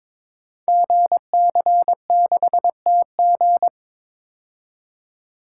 Morse code, heard as a keyed tone, about 22 wpm, 700 Hz. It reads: GC6TG
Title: Code Groups - 5 Letter:Numbers - 22wpm